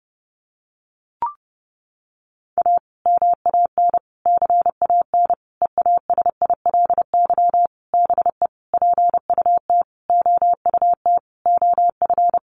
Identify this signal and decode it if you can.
Morse code, heard as a keyed tone, about 30 wpm, 700 Hz. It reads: A MAN CAN EASILY BE PUT OUT OF
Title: Sentences from Top 900 Words 30wpm